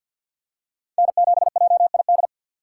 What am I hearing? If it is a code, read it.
Morse code, heard as a keyed tone, about 51 wpm, 700 Hz. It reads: N7JID
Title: USA Call Signs - 5 Characters - 2x Character Spacing - 50wpm